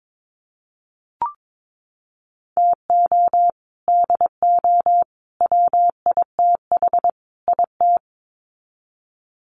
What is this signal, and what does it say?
Morse code, heard as a keyed tone, about 22 wpm, 700 Hz. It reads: TO DO WITH IT
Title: Sets of 4 Words 22wpm